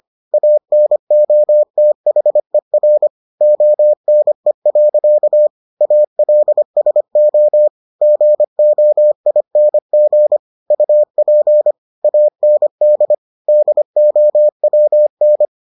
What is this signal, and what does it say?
Morse code, heard as a keyed tone, about 25 wpm, 600 Hz. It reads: ANOTHER ONE. ALSO GOING UP AND DOWN